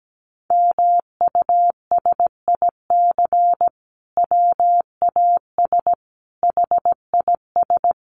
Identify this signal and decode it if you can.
Morse code, heard as a keyed tone, about 17 wpm, 700 Hz. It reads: MUSIC WAS HIS
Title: Sentences from Top 1300 Words 17wpm